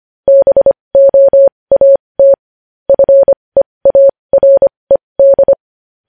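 Morse code, heard as a keyed tone, about 25 wpm, spaced as for 20 wpm, 550 Hz. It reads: BOAT FEARED
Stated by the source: Morse code at 20 wpm.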